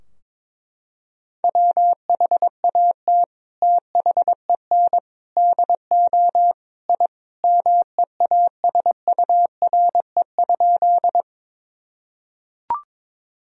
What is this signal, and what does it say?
Morse code, heard as a keyed tone, about 22 wpm, 700 Hz. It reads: WHAT THEN DO I MEASURE?
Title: Sentences from Top 1000 Words - Review Part 2 - 22wpm